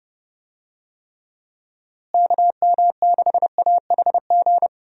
Morse code, heard as a keyed tone, about 30 wpm, 700 Hz. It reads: KM6AHG
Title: USA Call Signs - 6 Characters - Repeated 2x - 30wpm